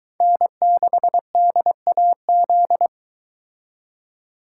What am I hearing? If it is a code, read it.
Morse code, heard as a keyed tone, about 23 wpm, 700 Hz. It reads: N6DAZ